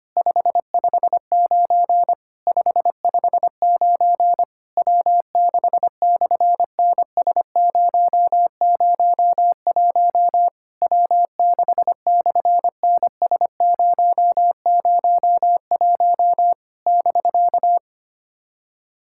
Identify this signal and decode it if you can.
Morse code, heard as a keyed tone, about 25 wpm, 700 Hz. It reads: 559 559 W6/NS001 W6/NS001 <BK>
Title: SOTA QSO Practice 25wpm